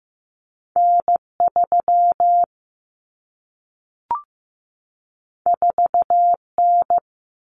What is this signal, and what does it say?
Morse code, heard as a keyed tone, about 15 wpm, 700 Hz. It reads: N 3 4 N
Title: Common Cut Numbers - 2-Digit 15wpm